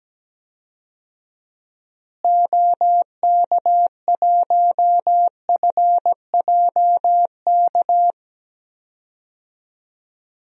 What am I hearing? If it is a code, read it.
Morse code, heard as a keyed tone, about 17 wpm, 700 Hz. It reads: OK1FJK
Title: DXCC-100 Call Signs - 6 Characters - Repeated 3x - 17wpm